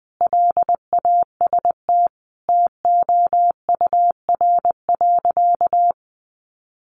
Morse code, heard as a keyed tone, about 20 wpm, 700 Hz. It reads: LAST TOUR.
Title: Sentences from Top 1400 Words 20wpm